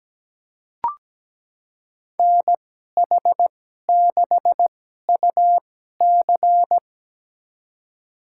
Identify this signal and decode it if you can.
Morse code, heard as a keyed tone, about 17 wpm, spaced as for 12 wpm, 700 Hz. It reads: NH6UC
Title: USA Call Signs - 5 Characters - 2x Character Spacing - 17wpm